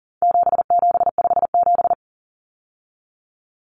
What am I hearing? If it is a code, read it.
Morse code, heard as a keyed tone, about 40 wpm, 700 Hz. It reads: 7757
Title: Numbers - 4-Digit 40wpm